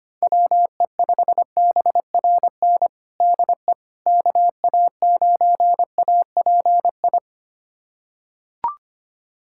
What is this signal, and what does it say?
Morse code, heard as a keyed tone, about 25 wpm, 700 Hz. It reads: WE5BRN DE KA9API
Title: USA Callsign DE Callsign - 6 Characters - 25wpm